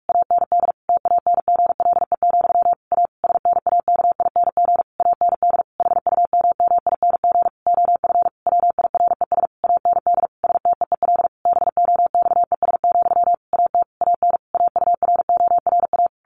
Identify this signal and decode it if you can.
Morse code, heard as a keyed tone, about 44 wpm, 700 Hz. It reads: AND TANGLE, A SNAKING AND HUMMING OF WIRES AND STEEL BOXES, AT AN AURORA
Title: A Sound Of Thunder 45wpm